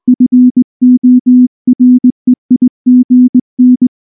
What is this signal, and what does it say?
Morse code, heard as a keyed tone, about 19 wpm, 250 Hz. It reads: FOREIGN
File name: morse / morse-training-20WPM-6